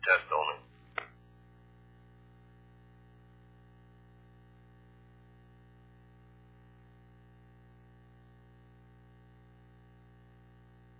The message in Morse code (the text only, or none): none